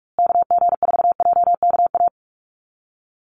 Morse code, heard as a keyed tone, about 44 wpm, 700 Hz. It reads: KG4JKA
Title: USA Call Signs Repeated 2x 45wpm